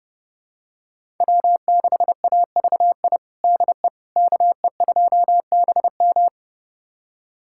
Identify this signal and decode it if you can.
Morse code, heard as a keyed tone, about 30 wpm, 700 Hz. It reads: W6AVI DE KE2BM
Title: USA Callsign DE Callsign - 5 Characters - 30wpm